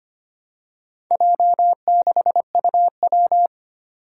Morse code, heard as a keyed tone, about 25 wpm, 700 Hz. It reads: J6UW
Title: Four Challenging Letter-Number Combinations 25wpm